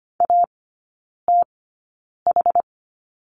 Morse code, heard as a keyed tone, about 25 wpm, spaced as for 9 wpm, 690 Hz. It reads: ATH
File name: Lesson 15e - Number 5 - Three Characters 25wpm